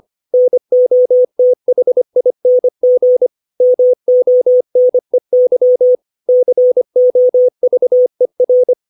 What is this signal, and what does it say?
Morse code, heard as a keyed tone, about 25 wpm, 500 Hz. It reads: NOTHING MONEY COVER
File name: 632 words / 632 Words 25WPM